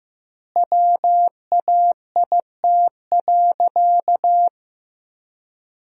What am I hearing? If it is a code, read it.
Morse code, heard as a keyed tone, about 15 wpm, 700 Hz. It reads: WAIT.
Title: 2x Word-Spacing - Sentences from Top 500 Words - 15wpm